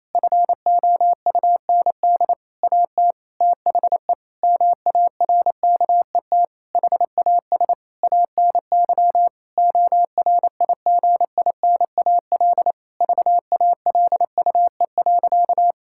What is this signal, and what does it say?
Morse code, heard as a keyed tone, about 28 wpm, 700 Hz. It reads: FOUND AT THE MARKET HAS ANY ORIGINAL VALUE.
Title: Sentences from Top 1000 Words - Review Part 6 - 28wpm